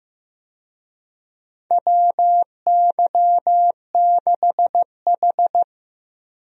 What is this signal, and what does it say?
Morse code, heard as a keyed tone, about 15 wpm, 700 Hz. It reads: WY6H
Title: USA Call Signs - 4 Characters - Repeated 3x - 15wpm